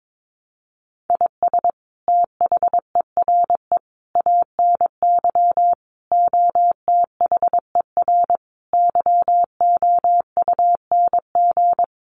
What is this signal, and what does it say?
Morse code, heard as a keyed tone, about 22 wpm, 700 Hz. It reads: IS THERE ANY OTHER YOUNG
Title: Sentences from Top 1100 Words 22wpm